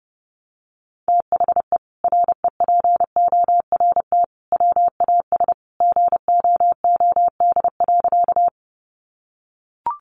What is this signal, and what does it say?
Morse code, heard as a keyed tone, about 30 wpm, 700 Hz. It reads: THE REPORT WAS GOOD.
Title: Sentences from Top 900 Words 30wpm